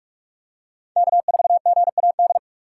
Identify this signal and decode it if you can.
Morse code, heard as a keyed tone, about 43 wpm, 700 Hz. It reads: K4GAD
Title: USA Call Signs - 2x Character Spacing - 45wpm